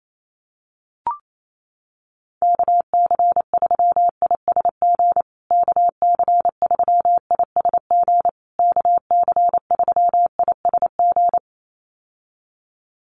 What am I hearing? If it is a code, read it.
Morse code, heard as a keyed tone, about 28 wpm, 700 Hz. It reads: KC3ISG KC3ISG KC3ISG
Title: USA Call Signs Repeated 3x 28wpm